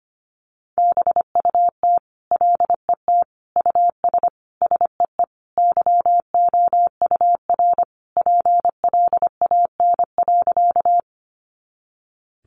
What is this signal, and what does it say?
Morse code, heard as a keyed tone, about 25 wpm, 700 Hz. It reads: BUT LET US SEE YOUR PLAN.
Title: Sentences from Top 500 Words 25wpm